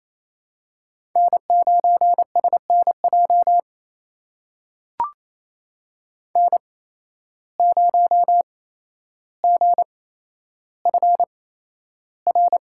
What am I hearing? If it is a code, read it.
Morse code, heard as a keyed tone, about 28 wpm, spaced as for 12 wpm, 700 Hz. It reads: N9SNJ N0GFR
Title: USA Call Signs - 5 Characters - 8x Character Spacing - 28wpm